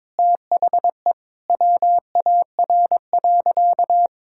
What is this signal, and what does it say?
Morse code, heard as a keyed tone, about 22 wpm, 700 Hz. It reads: THE WAR.
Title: Sentences from Top 1000 Words - Review Part 6 - 22wpm